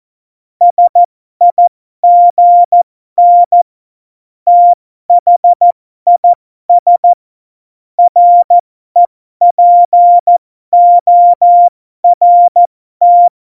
Morse code, heard as a keyed tone, about 14 wpm, spaced as for 12 wpm, 700 Hz. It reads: SIGN THIS REPORT
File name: Morsekurs / Witze